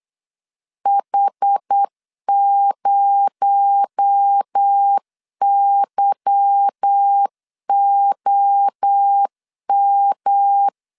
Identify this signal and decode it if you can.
Morse code, heard as a keyed tone, about 8 wpm, 800 Hz. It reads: H0YOM
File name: LEZIONI / 10 SIGNS / 10A SLASH 8wpm 3452